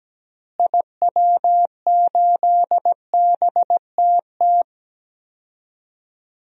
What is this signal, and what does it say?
Morse code, heard as a keyed tone, about 17 wpm, 700 Hz. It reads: IW8BTT
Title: DXCC-100 Call Signs - 6 Characters - Repeated 3x - 17wpm